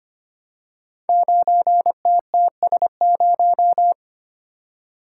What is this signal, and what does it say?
Morse code, heard as a keyed tone, about 25 wpm, 700 Hz. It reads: 9TTS0
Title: Code Groups - 5 Letter:Numbers - 25wpm